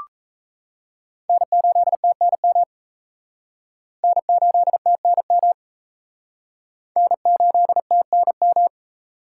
Morse code, heard as a keyed tone, about 34 wpm, 700 Hz. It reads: N8TNM N8TNM N8TNM
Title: Speed Racing - USA 1x3 Call Signs 28wpm